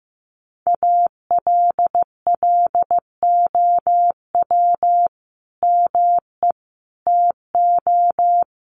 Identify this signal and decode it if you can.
Morse code, heard as a keyed tone, about 15 wpm, 700 Hz. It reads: ALLOW ME TO
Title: Sentences from Top 1300 Words 15wpm